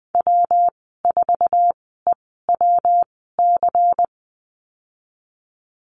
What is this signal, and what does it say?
Morse code, heard as a keyed tone, about 20 wpm, spaced as for 15 wpm, 700 Hz. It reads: W4EWC
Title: 173 Lesson 28g - Number 9 - Call Signs 20wpm